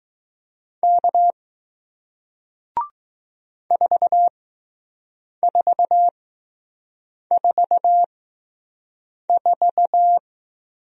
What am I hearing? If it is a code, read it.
Morse code, heard as a keyed tone, about 20 wpm, spaced as for 6 wpm, 700 Hz. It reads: K 4444